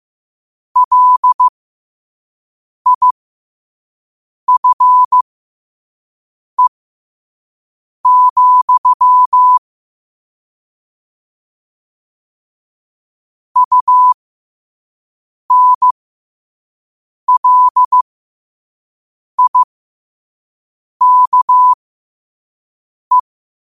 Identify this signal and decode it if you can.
Morse code, heard as a keyed tone, about 15 wpm, spaced as for 5 wpm, 1000 Hz. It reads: LIFE, UNLIKE
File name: qotdfiles / 5wpm